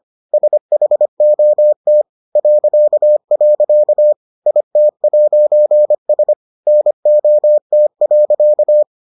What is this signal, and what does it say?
Morse code, heard as a keyed tone, about 25 wpm, 600 Hz. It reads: SHOT .. IT'S NOT.